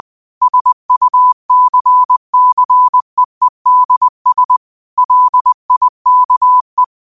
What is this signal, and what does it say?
Morse code, heard as a keyed tone, about 20 wpm, 1000 Hz. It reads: SUCCEEDS LIKE
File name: qotdfiles / 20wpm